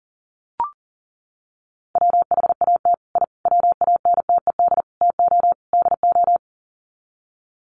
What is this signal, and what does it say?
Morse code, heard as a keyed tone, about 40 wpm, 700 Hz. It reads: WHAT I WANTED TO DO
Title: Sets of 5 Words Encore2 40wpm